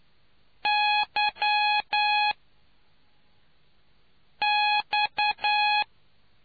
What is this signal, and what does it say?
Morse code, heard as a keyed tone, about 10 wpm, spaced as for 3 wpm, 840 Hz. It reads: YX